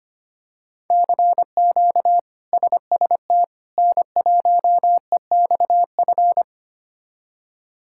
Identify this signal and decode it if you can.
Morse code, heard as a keyed tone, about 25 wpm, 700 Hz. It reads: CQ SST N1EXF
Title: K1USN SST QSO Practice - 25wpm